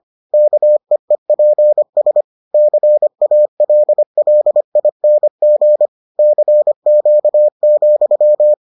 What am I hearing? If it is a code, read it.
Morse code, heard as a keyed tone, about 25 wpm, 600 Hz. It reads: KEEPS CALLING CQ,